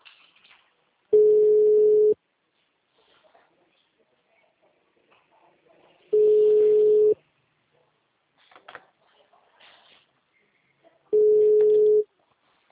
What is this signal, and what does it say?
Ringback tone (Europe, 425 Hz).